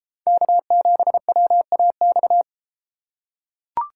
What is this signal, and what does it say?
Morse code, heard as a keyed tone, about 31 wpm, 700 Hz. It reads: K7WAX